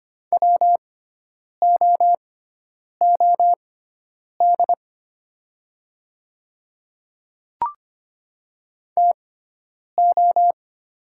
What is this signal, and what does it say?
Morse code, heard as a keyed tone, about 25 wpm, spaced as for 9 wpm, 700 Hz. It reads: WOOD TO
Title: 6x Character Spacing - Top 300 Words - 25wpm